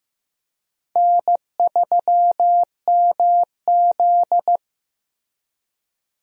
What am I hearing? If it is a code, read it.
Morse code, heard as a keyed tone, about 15 wpm, 700 Hz. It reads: N3MZ
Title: USA Call Signs - 4 Characters - Repeated 2x - 15wpm